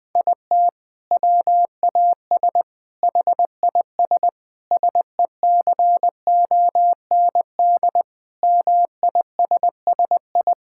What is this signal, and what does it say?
Morse code, heard as a keyed tone, about 21 wpm, 700 Hz. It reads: IT WAS HIS SECOND MISSI
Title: Sentences from Top 1600 Words 20wpm